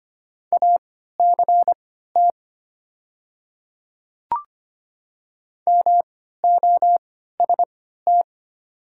Morse code, heard as a keyed tone, about 25 wpm, spaced as for 14 wpm, 690 Hz. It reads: ACT MOST
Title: 3x Character Spacing - Top 300 Words - 25wpm